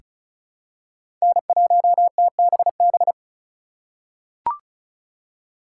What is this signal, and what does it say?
Morse code, heard as a keyed tone, about 34 wpm, 700 Hz. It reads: N1TBB
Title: USA Call Signs - 5 Characters - 3x Character Spacing - 35wpm